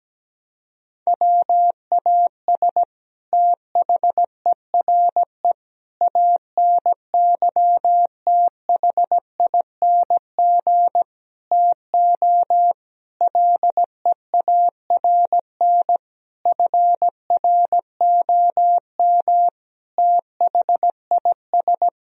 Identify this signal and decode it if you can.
Morse code, heard as a keyed tone, about 17 wpm, 700 Hz. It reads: WAS THERE ANYTHING TO LEARN FROM THIS
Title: Sentences from Top 1000 Words - Review Part 1 - 17wpm